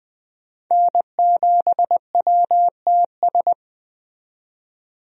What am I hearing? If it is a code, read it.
Morse code, heard as a keyed tone, about 20 wpm, 700 Hz. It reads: N7WTS
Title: USA Call Signs - 5 Characters - 20wpm